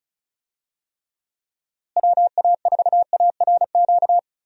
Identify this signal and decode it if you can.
Morse code, heard as a keyed tone, about 36 wpm, 700 Hz. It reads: WA4ARQ
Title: USA Call Signs - 6 Characters - Repeated 3x - 35wpm